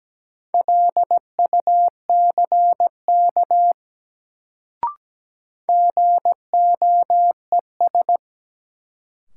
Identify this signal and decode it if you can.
Morse code, heard as a keyed tone, about 17 wpm, 700 Hz. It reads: LUCK GOES
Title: Sentences from Top 1500 Words - New Word Drill Alt - 17wpm